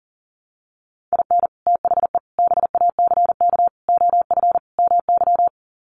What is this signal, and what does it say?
Morse code, heard as a keyed tone, about 40 wpm, 700 Hz. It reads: IN THE BACK OF MY
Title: Sets of 5 Words Encore 40wpm